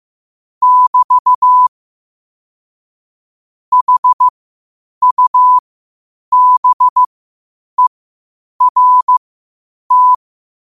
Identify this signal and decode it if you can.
Morse code, heard as a keyed tone, about 15 wpm, spaced as for 8 wpm, 1000 Hz. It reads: <BT> HUBERT
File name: qotdfiles / 8wpm